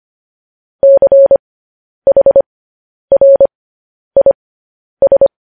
Morse code, heard as a keyed tone, about 25 wpm, spaced as for 10 wpm, 550 Hz. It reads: CHRIS
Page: news headlines in morse code at 10 wpm